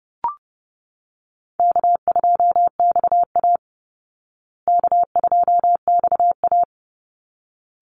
Morse code, heard as a keyed tone, about 30 wpm, 700 Hz. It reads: K2XA K2XA
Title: Easy Call Signs Repeated 2x - 30wpm